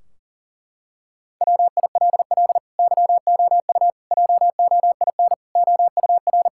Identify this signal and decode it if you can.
Morse code, heard as a keyed tone, about 40 wpm, 700 Hz. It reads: WILL YOU JOIN OUR
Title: Sentences from Top 1600 Words 40wpm